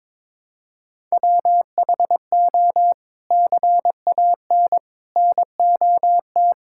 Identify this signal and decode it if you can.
Morse code, heard as a keyed tone, about 22 wpm, 700 Hz. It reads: WHO CAN NOT
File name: practice / 22wpm / Sets of 8 Words - 22wpm